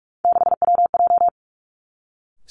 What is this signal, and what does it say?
Morse code, heard as a keyed tone, about 46 wpm, 700 Hz. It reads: 6WJ
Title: Three Challenging Letter-Number Combinations 45wpm